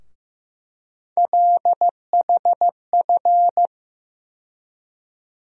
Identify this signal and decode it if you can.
Morse code, heard as a keyed tone, about 15 wpm, 700 Hz. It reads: LHF
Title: USA Call Sign Postfixes - 3 Letters 15wpm